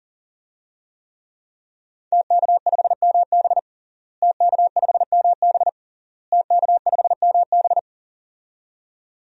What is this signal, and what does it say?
Morse code, heard as a keyed tone, about 40 wpm, 700 Hz. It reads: TK5MB TK5MB TK5MB
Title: DXCC-100 Call Signs - 5 Characters - Repeated 3x - 40wpm